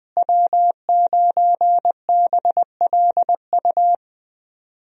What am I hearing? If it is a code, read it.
Morse code, heard as a keyed tone, about 20 wpm, 700 Hz. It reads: W9BLU
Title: USA Call Signs - 5 Characters - Repeated 2x - 20wpm